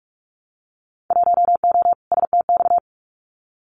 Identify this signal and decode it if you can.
Morse code, heard as a keyed tone, about 44 wpm, 700 Hz. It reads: 1O STX
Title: Winter Field Day Exchanges 45wpm